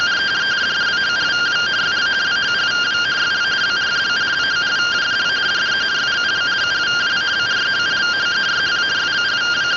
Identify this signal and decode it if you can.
Two-tone teleprinter signal, about 170 Hz shift.